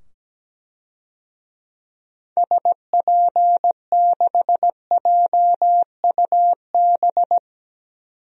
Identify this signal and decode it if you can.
Morse code, heard as a keyed tone, about 17 wpm, 700 Hz. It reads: SP6JUB
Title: DXCC-100 Call Signs - 6 Characters - 17wpm